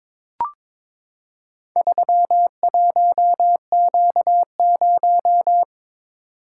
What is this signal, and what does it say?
Morse code, heard as a keyed tone, about 21 wpm, 700 Hz. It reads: 31Q0
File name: practice / 22wpm / Four Letter-Number Combinations - 22wpm